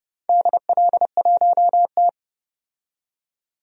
Morse code, heard as a keyed tone, about 31 wpm, 700 Hz. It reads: DL1T
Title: DXCC-100 Call Signs - 4 Characters - 30wpm